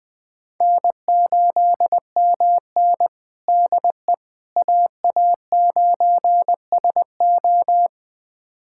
Morse code, heard as a keyed tone, about 20 wpm, 700 Hz. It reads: N8MN DE AA9SO
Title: USA Callsign DE Callsign 20wpm